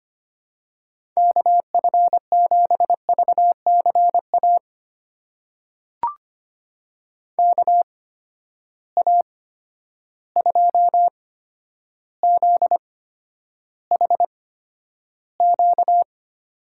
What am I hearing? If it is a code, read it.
Morse code, heard as a keyed tone, about 25 wpm, spaced as for 11 wpm, 700 Hz. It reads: KF7VCA KA2ZHQ
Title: USA Call Signs - 6 Characters - 8x Character Spacing - 25wpm